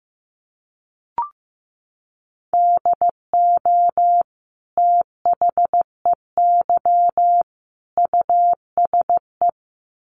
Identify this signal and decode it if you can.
Morse code, heard as a keyed tone, about 15 wpm, 700 Hz. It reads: DO THEY USE
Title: Sentences from Top 1300 Words 15wpm